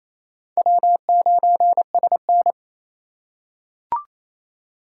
Morse code, heard as a keyed tone, about 28 wpm, 700 Hz. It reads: W9SN
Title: USA Call Signs - 4 Characters - 2x Character Spacing - 28wpm